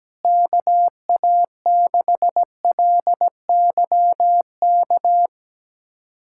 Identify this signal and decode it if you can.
Morse code, heard as a keyed tone, about 17 wpm, 700 Hz. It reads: KA6LYK
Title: USA Callsign DE Callsign 17wpm